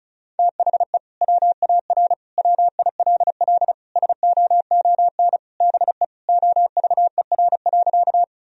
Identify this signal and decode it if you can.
Morse code, heard as a keyed tone, about 36 wpm, 700 Hz. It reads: THE WAR WILL SOON BE OVER.
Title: Sentences from Top 1000 Words - Review Part 2 - 35wpm